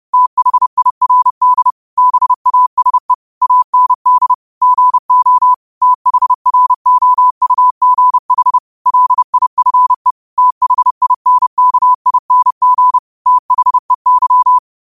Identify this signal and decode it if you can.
Morse code, heard as a keyed tone, about 30 wpm, 1000 Hz. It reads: THIRD BASE AND GO THROUGH LIFE THINKING THEY
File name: qotdfiles / 30wpm